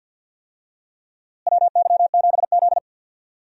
Morse code, heard as a keyed tone, about 53 wpm, 700 Hz. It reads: WY7Z
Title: USA Call Signs - 4 Characters - Repeated 2x - 50wpm